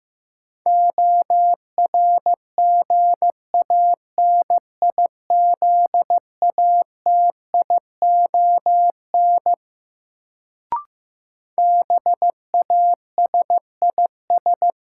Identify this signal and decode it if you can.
Morse code, heard as a keyed tone, about 15 wpm, 700 Hz. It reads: ORGANIZATION BASIS
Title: Sentences from Top 1000 Words - Review Part 5 - New Word Drill - 15wpm